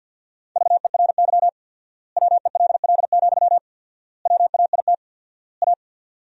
Morse code, heard as a keyed tone, about 50 wpm, 700 Hz. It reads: VERY WELL, WAIT A
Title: 4x Word-Spacing - Sentences from Top 500 Words - 50wpm